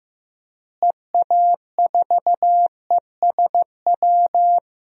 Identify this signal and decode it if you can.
Morse code, heard as a keyed tone, about 15 wpm, 700 Hz. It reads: EA4ESW
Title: DXCC-100 Call Signs - 6 Characters - Repeated 2x - 15wpm